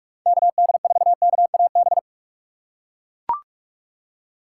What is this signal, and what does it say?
Morse code, heard as a keyed tone, about 42 wpm, 700 Hz. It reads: KD4KAB
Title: USA Call Signs - 6 Characters - 4x Character Spacing - 45wpm